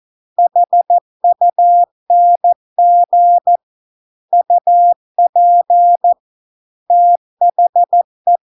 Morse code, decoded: HUNG UP THE

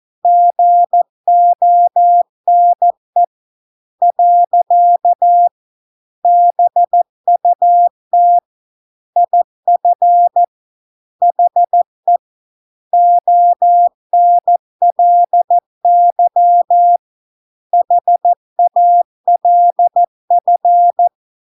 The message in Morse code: GONE . BUT IF HE ONLY HALF

